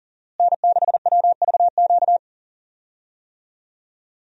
Morse code, heard as a keyed tone, about 37 wpm, 700 Hz. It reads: N6WVQ
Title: USA Call Signs - 5 Characters - Repeated 3x - 40wpm